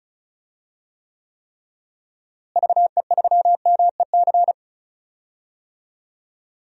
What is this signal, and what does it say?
Morse code, heard as a keyed tone, about 34 wpm, 700 Hz. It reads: VE3MEC
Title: DXCC-100 Call Signs - 6 Characters - Repeated 3x - 35wpm